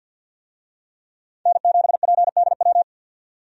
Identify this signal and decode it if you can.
Morse code, heard as a keyed tone, about 52 wpm, 700 Hz. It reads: N7PDW
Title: USA Call Signs - 5 Characters - 50wpm